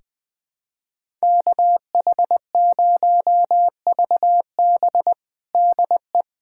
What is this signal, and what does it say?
Morse code, heard as a keyed tone, about 20 wpm, 700 Hz. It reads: KH0VB DE
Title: USA Callsign DE Callsign - 5 Characters - 20wpm